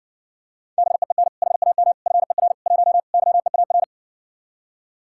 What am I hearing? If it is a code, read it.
Morse code, heard as a keyed tone, about 58 wpm, 700 Hz. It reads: BEEN HAM FER 1 YEAR